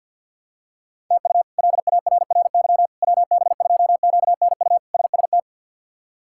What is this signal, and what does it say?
Morse code, heard as a keyed tone, about 48 wpm, 700 Hz. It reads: TU LARRY WB2QNU SST
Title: K1USN SST QSO Practice - 50wpm